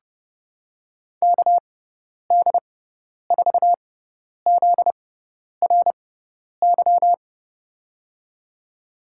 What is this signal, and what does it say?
Morse code, heard as a keyed tone, about 29 wpm, spaced as for 10 wpm, 700 Hz. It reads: KD4ZRY
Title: USA Call Signs - 6 Characters - 6x Character Spacing - 30wpm